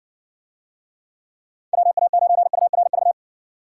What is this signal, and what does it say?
Morse code, heard as a keyed tone, about 68 wpm, 700 Hz. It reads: WA9URV